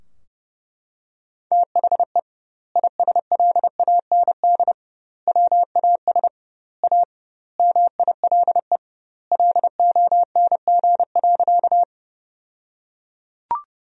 Morse code, heard as a keyed tone, about 30 wpm, 700 Hz. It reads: THE ISLAND WAS A MILE LONG.
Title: 2x Word-Spacing - Sentences from Top 500 Words - 30wpm